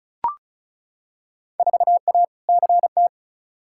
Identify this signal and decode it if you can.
Morse code, heard as a keyed tone, about 36 wpm, 700 Hz. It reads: 4A CT
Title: Field Day Exchanges Encore - 35wpm